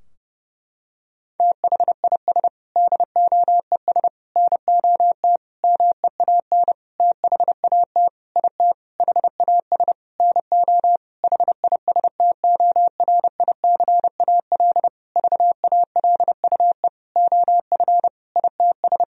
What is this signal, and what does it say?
Morse code, heard as a keyed tone, about 32 wpm, 700 Hz. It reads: THIS DOES NOT MEAN THAT IT HAS NO HISTORICAL VALUE OF ITS
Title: Sentences from Top 1000 Words - Review Part 4 - 30wpm